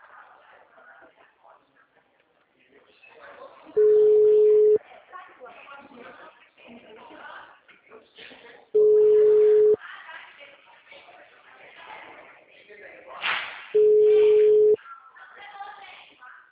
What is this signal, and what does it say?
Ringback tone (Europe, 425 Hz), with speech.
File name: out- -2004- - - .176